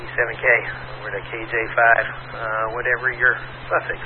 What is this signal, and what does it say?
Emergency Alert System (SAME) tones, heard as harsh buzzing bursts, with speech. It is heard from 0.4 to 3.4 s.